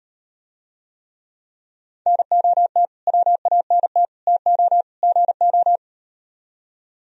Morse code, heard as a keyed tone, about 39 wpm, 700 Hz. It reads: NOT WANT TO GO